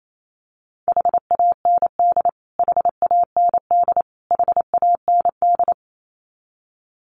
Morse code, heard as a keyed tone, about 28 wpm, 700 Hz. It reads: HAND HAND HAND
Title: Top 500 Words Repeated 28wpm